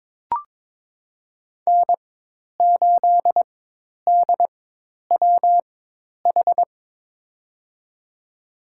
Morse code, heard as a keyed tone, about 22 wpm, spaced as for 10 wpm, 700 Hz. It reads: N8DWH
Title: USA Call Signs - 5 Characters - 4x Character Spacing - 22wpm